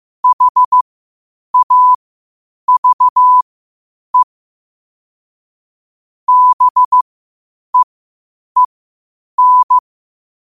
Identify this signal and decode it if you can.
Morse code, heard as a keyed tone, about 15 wpm, spaced as for 8 wpm, 1000 Hz. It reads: HAVE BEEN
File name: qotdfiles / 8wpm